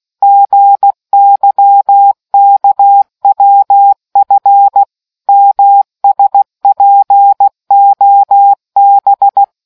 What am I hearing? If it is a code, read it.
Morse code, heard as a keyed tone, about 16 wpm, 800 Hz. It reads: GYKWF MSPOB